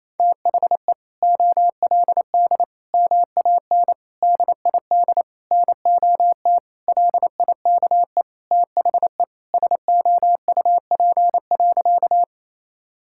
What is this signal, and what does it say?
Morse code, heard as a keyed tone, about 28 wpm, 700 Hz. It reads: THE OLD MAN DID NOT LIKE THE SOUP.
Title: Sentences from Top 1000 Words - Review Part 6 - 28wpm